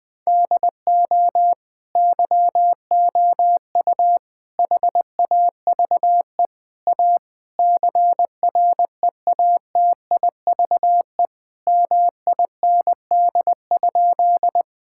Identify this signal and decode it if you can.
Morse code, heard as a keyed tone, about 20 wpm, 700 Hz. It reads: DO YOU HAVE A CREATIVE MIND?
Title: Sentences from Top 1000 Words - Review Part 5 - 20wpm